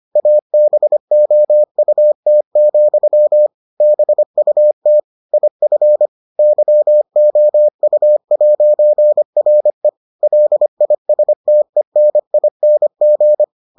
Morse code, heard as a keyed tone, about 25 wpm, 600 Hz. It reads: ABOUT, BUT IF YOU'RE LISTENING